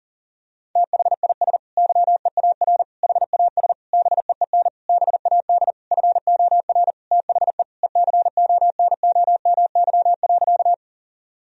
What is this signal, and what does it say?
Morse code, heard as a keyed tone, about 40 wpm, 700 Hz. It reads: THIS YEAR HAS BEEN BAD FOR THE ECONOMY.
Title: Sentences from Top 1000 Words - Review Part 3 - 40wpm